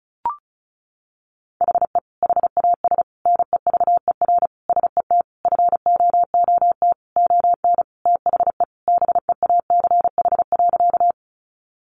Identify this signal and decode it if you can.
Morse code, heard as a keyed tone, about 34 wpm, 700 Hz. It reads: HE HAS NEVER SET FOOT ON THE BEACH.
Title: Sentences from Top 1500 Words 35wpm